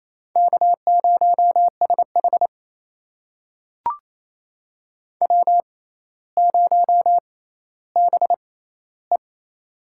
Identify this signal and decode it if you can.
Morse code, heard as a keyed tone, about 28 wpm, spaced as for 14 wpm, 700 Hz. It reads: K0SH W0BE
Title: USA Call Signs - 4 Characters - 6x Character Spacing - 28wpm